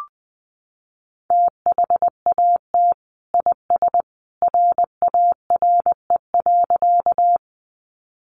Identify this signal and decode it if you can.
Morse code, heard as a keyed tone, about 20 wpm, 700 Hz. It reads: THAT IS RARE.
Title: Sentences from Top 1100 Words 20wpm